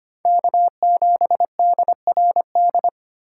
Morse code, heard as a keyed tone, about 25 wpm, 700 Hz. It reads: K7DRD
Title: USA Call Signs - 3x Character Spacing - 25wpm